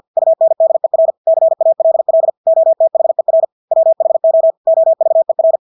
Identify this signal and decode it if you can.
Morse code, heard as a keyed tone, about 50 wpm, 650 Hz. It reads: UNDER CALL OTHER WHO OVER
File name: Engelske ord 50 wpm